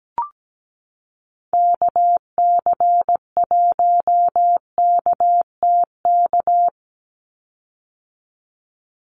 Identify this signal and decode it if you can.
Morse code, heard as a keyed tone, about 17 wpm, 700 Hz. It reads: KC1KTK
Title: USA Call Signs 17wpm